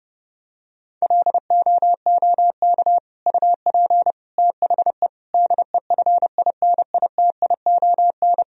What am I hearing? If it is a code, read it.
Morse code, heard as a keyed tone, about 30 wpm, 700 Hz. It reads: LOOK UP THE DEFINITION
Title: Sentences from Top 1000 Words - Review Part 4 - 30wpm